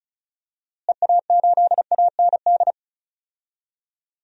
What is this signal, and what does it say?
Morse code, heard as a keyed tone, about 34 wpm, 700 Hz. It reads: EA8AND
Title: DXCC-100 Call Signs - 6 Characters - Repeated 2x - 35wpm